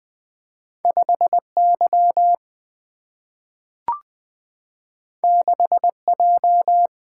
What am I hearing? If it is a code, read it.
Morse code, heard as a keyed tone, about 20 wpm, 700 Hz. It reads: 5Y 6J
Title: Single-Letter Single-Number 20wpm